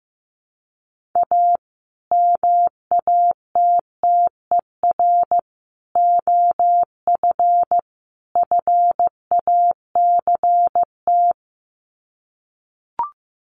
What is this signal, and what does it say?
Morse code, heard as a keyed tone, about 15 wpm, 700 Hz. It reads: A MATTER OF FACT
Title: Sets of 4 Words 15wpm